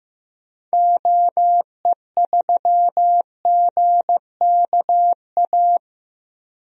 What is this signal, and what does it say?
Morse code, heard as a keyed tone, about 15 wpm, 700 Hz. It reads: OE3GKA
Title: DXCC-100 Call Signs - 6 Characters - Repeated 2x - 15wpm